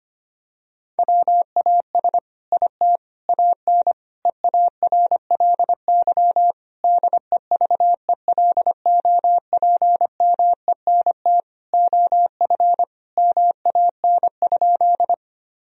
Morse code, heard as a keyed tone, about 25 wpm, 700 Hz. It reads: WAS IT AN EARLY DEVELOPMENT OF MAN?
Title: Sentences from Top 1000 Words - Review Part 4 - 25wpm